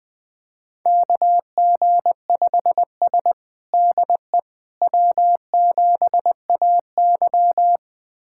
Morse code, heard as a keyed tone, about 20 wpm, 700 Hz. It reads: KG5S DE W7AY
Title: USA Callsign DE Callsign - 4 Characters - 20wpm